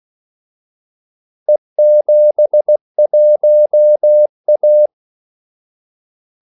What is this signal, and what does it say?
Morse code, heard as a keyed tone, about 16 wpm, 600 Hz. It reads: E71A
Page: Kaldesignaler 16 wpm